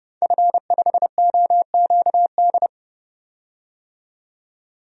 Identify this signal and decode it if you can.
Morse code, heard as a keyed tone, about 30 wpm, 700 Hz. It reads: F5OQD
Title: DXCC-100 Call Signs - 5 Characters - 30wpm